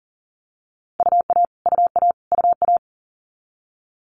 Morse code, heard as a keyed tone, about 40 wpm, 690 Hz. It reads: UA UA UA
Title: Multi-letter Phonemes Repeated 3x 40wpm